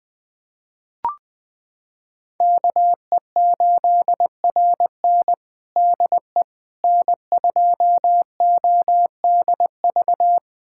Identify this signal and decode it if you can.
Morse code, heard as a keyed tone, about 20 wpm, 700 Hz. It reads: KE8RN DE N2ODV
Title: USA Callsign DE Callsign - 5 Characters - 20wpm